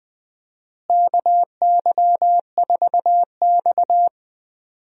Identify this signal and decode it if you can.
Morse code, heard as a keyed tone, about 21 wpm, 700 Hz. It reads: KY4X
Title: USA Call Signs - 4 Characters - Repeated 3x - 20wpm